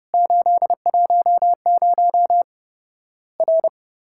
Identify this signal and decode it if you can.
Morse code, heard as a keyed tone, about 30 wpm, 700 Hz. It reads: 810 R
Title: Worked All Europe DX Contest - QTC Exchange - 30wpm